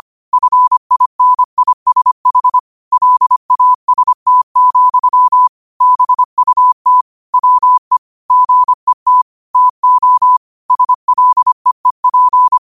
Morse code, heard as a keyed tone, about 25 wpm, 1000 Hz. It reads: FINISH LAST, BUT WE GET TO SLEEP
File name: qotdfiles / 25wpm